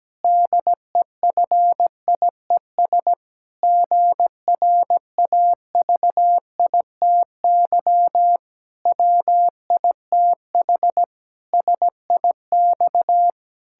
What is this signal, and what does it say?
Morse code, heard as a keyed tone, about 17 wpm, 700 Hz. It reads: DEFIES GRAVITY WITH SIX
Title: Pangram Challenge v2 - 17wpm